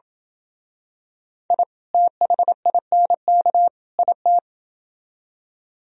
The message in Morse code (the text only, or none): I THINK IT